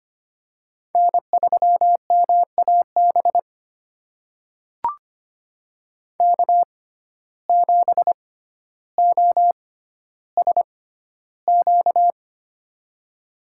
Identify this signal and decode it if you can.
Morse code, heard as a keyed tone, about 25 wpm, spaced as for 13 wpm, 700 Hz. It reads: N3MAB K7OSQ
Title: USA Call Signs - 5 Characters - 6x Character Spacing - 25wpm